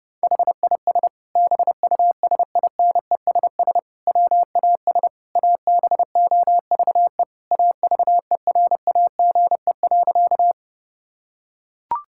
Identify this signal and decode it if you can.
Morse code, heard as a keyed tone, about 29 wpm, 700 Hz. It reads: HIS BUSINESS WAS ABOVE AVERAGE.
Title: Sentences from Top 1000 Words - Review Part 6 - 30wpm